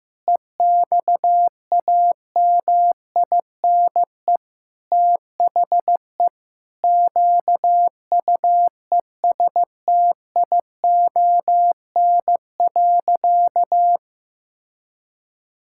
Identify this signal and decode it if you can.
Morse code, heard as a keyed tone, about 15 wpm, 700 Hz. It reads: EXAMINE THE QUESTION.
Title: Sentences from Top 1600 Words 15wpm